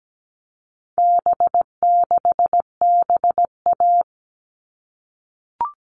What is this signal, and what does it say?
Morse code, heard as a keyed tone, about 17 wpm, 700 Hz. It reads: B6BA
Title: All Cut Numbers - 4-Digit 17wpm